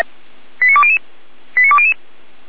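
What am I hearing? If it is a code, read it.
Five-tone selective call (ZVEI), heard as quick runs of five tones, tones of about 70 ms. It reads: 89200 89200
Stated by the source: Five-tone ZVEI selective call received on an SDR.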